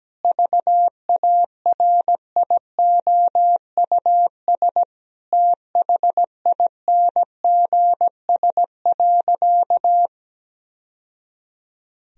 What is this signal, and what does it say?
Morse code, heard as a keyed tone, about 17 wpm, 700 Hz. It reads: VARIOUS THINGS.
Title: Sentences from Top 1000 Words - Review Part 4 - 17wpm